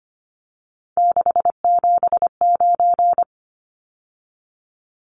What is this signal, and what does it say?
Morse code, heard as a keyed tone, about 25 wpm, 700 Hz. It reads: 679
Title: Numbers - 3-Digit 25wpm